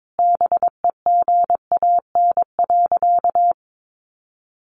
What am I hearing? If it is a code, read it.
Morse code, heard as a keyed tone, about 22 wpm, 700 Hz. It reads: BEGAN.
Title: Sentences from Top 1100 Words 22wpm